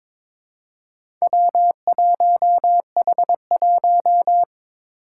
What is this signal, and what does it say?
Morse code, heard as a keyed tone, about 22 wpm, 700 Hz. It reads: W1H1
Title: Four Challenging Letter-Number Combinations 22wpm